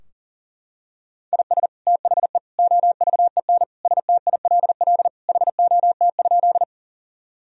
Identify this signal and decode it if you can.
Morse code, heard as a keyed tone, about 40 wpm, 700 Hz. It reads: IS THE OVEN STILL HOT?
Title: Sentences from Top 1000 Words - Review Part 4 - 40wpm